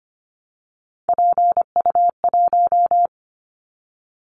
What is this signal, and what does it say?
Morse code, heard as a keyed tone, about 25 wpm, 700 Hz. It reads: PU1
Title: Callsign Prefixes for Top 50 DXCC - 25wpm